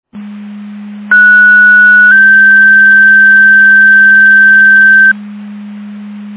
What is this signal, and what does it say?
Two-tone page, heard as a one-second tone, then a three-second tone, about 1470 then 1550 Hz. It is heard from 1.1 to 5.1 s.
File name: Portage County 154.3850 Fire EMS Paging